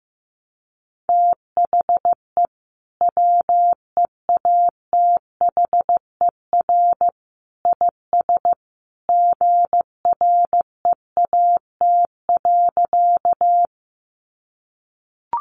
Morse code, heard as a keyed tone, about 15 wpm, 700 Hz. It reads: THE WEATHER IS GREAT.
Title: Sentences from Top 800 Words 15wpm